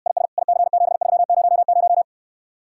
Morse code, heard as a keyed tone, about 69 wpm, 700 Hz. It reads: IS IC7300